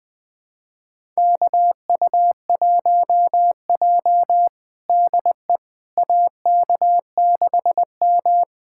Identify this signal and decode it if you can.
Morse code, heard as a keyed tone, about 21 wpm, 700 Hz. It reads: KU1J DE AK6M
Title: USA Callsign DE Callsign - 4 Characters - 20wpm